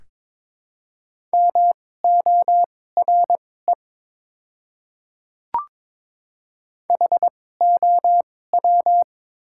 Morse code, heard as a keyed tone, about 22 wpm, spaced as for 16 wpm, 700 Hz. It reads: MORE HOW
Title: 2x Character Spacing - Top 100 Words - 22wpm